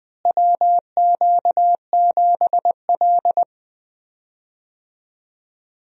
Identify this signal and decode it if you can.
Morse code, heard as a keyed tone, about 20 wpm, 700 Hz. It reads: WQ7L
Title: USA Call Signs - 4 Characters - Repeated 2x - 20wpm